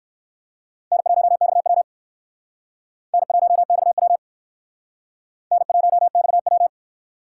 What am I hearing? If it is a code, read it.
Morse code, heard as a keyed tone, about 61 wpm, 700 Hz. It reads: N1XW N1XW N1XW